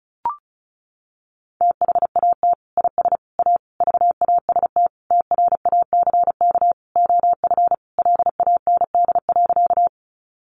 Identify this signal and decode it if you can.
Morse code, heard as a keyed tone, about 34 wpm, 700 Hz. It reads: THAT IS A VAST TRACK OF LAND.
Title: Sentences from Top 1300 Words 35wpm